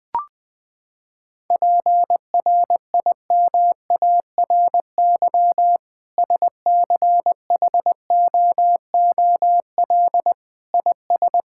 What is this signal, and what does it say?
Morse code, heard as a keyed tone, about 20 wpm, 700 Hz. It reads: PRIMARY SCHOOL IS
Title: Sentences from Top 1000 Words - Review Part 5 - 20wpm